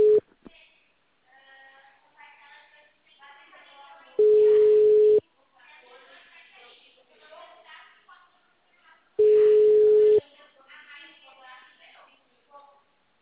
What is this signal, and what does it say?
Ringback tone (Europe, 425 Hz), with speech.
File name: out- -2020- - - .71656